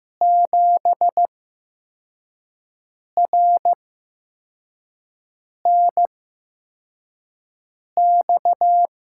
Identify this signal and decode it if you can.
Morse code, heard as a keyed tone, about 15 wpm, spaced as for 4 wpm, 700 Hz. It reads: 7RNX
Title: USA Call Signs - 5 Characters - 8x Character Spacing - 15wpm